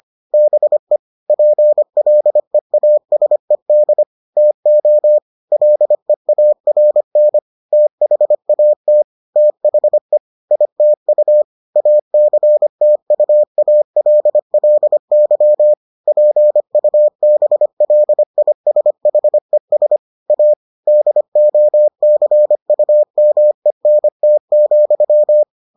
Morse code, heard as a keyed tone, about 25 wpm, 600 Hz. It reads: BE PLEASED TO LEARN THAT THE ITU ACTUALLY PUBLISHES A DOCUMENT,